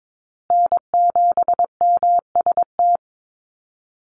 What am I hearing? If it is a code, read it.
Morse code, heard as a keyed tone, about 22 wpm, 700 Hz. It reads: N7MST
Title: USA Call Signs Repeated 2x 22wpm